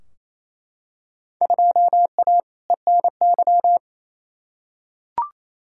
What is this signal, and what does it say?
Morse code, heard as a keyed tone, about 28 wpm, 700 Hz. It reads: 2A ENY
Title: Field Day Exchanges Encore - 28wpm